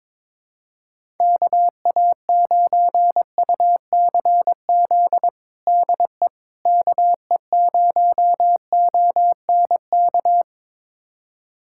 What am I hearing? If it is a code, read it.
Morse code, heard as a keyed tone, about 22 wpm, 700 Hz. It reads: KA9UCZ DE KE0ONK
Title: USA Callsign DE Callsign - 6 Characters - 22wpm